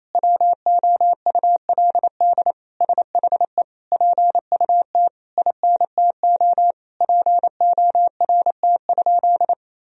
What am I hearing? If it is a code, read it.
Morse code, heard as a keyed tone, about 28 wpm, 700 Hz. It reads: WOULD SHE PUT INTO PORT?
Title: Sentences from Top 1000 Words - Review Part 1 - 28wpm